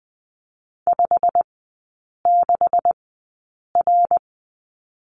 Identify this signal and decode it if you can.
Morse code, heard as a keyed tone, about 20 wpm, spaced as for 8 wpm, 700 Hz. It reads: 56R
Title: 184 Lesson 30e - Letter b - Three Characters 20wpm